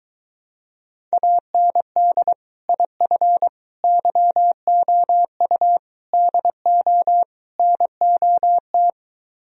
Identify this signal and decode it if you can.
Morse code, heard as a keyed tone, about 22 wpm, 700 Hz. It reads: AND IF YOU DO NOT